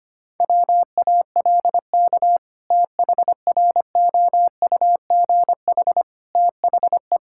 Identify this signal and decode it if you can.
Morse code, heard as a keyed tone, about 25 wpm, 700 Hz. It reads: WALK THROUGH THE
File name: practice / 25wpm / Sentences with 2-Word Phrases - 25wpm